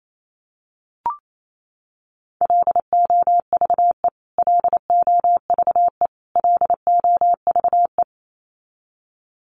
Morse code, heard as a keyed tone, about 28 wpm, 700 Hz. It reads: LOVE LOVE LOVE
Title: Top 500 Words Repeated 28wpm